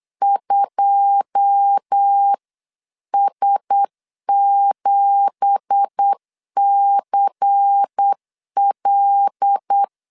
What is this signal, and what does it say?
Morse code, heard as a keyed tone, about 8 wpm, 800 Hz. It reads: 2S7CL
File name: 8wpm